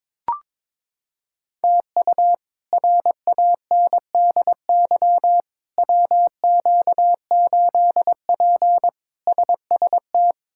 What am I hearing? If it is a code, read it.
Morse code, heard as a keyed tone, about 22 wpm, 700 Hz. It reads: TU RANDY WQ8P SST
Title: K1USN SST QSO Practice - 22wpm